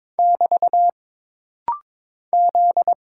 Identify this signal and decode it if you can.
Morse code, heard as a keyed tone, about 22 wpm, 700 Hz. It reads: <BT> Z
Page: This rapid-fire timing is geared to push your ICR (Instant Character Recognition) skills.